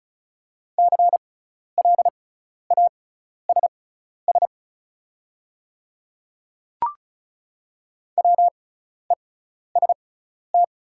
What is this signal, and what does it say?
Morse code, heard as a keyed tone, about 36 wpm, spaced as for 12 wpm, 690 Hz. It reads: CLASS WEST
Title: 6x Character Spacing - Top 500 Words - 35wpm